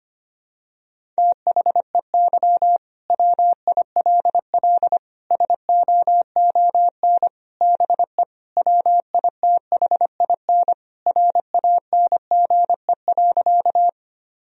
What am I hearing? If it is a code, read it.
Morse code, heard as a keyed tone, about 25 wpm, 700 Hz. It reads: THEY WILL SOON BE WITHIN RANGE.
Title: Sentences from Top 1000 Words - Review Part 4 - 25wpm